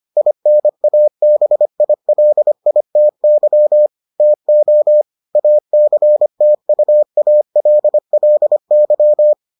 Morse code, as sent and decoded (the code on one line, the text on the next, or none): .. -. .- -... .. .-.. .. - -.-- / - --- / .- -.-. - ..- .- .-.. .-.. -.--
INABILITY TO ACTUALLY